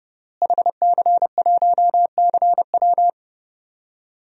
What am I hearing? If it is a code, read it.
Morse code, heard as a keyed tone, about 30 wpm, 700 Hz. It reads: HC1CW
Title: DXCC-100 Call Signs - 5 Characters - Repeated 3x - 30wpm